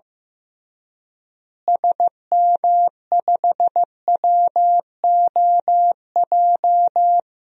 Morse code, heard as a keyed tone, about 15 wpm, 700 Hz. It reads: SM5WOJ
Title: DXCC-100 Call Signs - 6 Characters - Repeated 3x - 15wpm